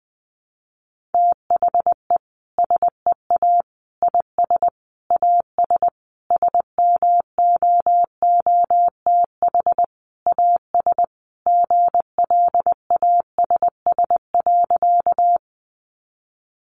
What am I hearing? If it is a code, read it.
Morse code, heard as a keyed tone, about 20 wpm, 700 Hz. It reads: THE SEA IS AS SMOOTH AS GLASS.
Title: Sentences from Top 1400 Words 20wpm